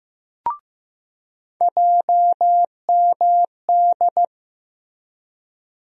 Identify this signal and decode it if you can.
Morse code, heard as a keyed tone, about 16 wpm, 700 Hz. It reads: JMD
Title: USA Call Sign Postfixes - 3 Letters 15wpm